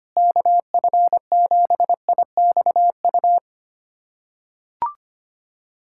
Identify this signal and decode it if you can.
Morse code, heard as a keyed tone, about 25 wpm, 700 Hz. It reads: KF7IXU
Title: USA Callsign DE Callsign - 6 Characters - 25wpm